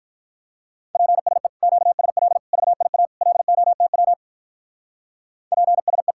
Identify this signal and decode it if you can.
Morse code, heard as a keyed tone, about 44 wpm, 700 Hz. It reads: PSE QSL VIA LOTW PSE